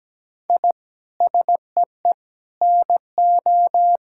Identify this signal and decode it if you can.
Morse code, heard as a keyed tone, about 17 wpm, 700 Hz. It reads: I SEE NO
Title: Sentences from Top 1000 Words - Review Part 5 - 17wpm